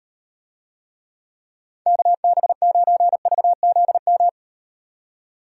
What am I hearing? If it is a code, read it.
Morse code, heard as a keyed tone, about 38 wpm, 700 Hz. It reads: KB9VZM